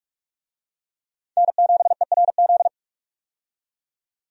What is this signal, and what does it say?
Morse code, heard as a keyed tone, about 46 wpm, 700 Hz. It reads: N7ERZ
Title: USA Call Signs - 5 Characters - 45wpm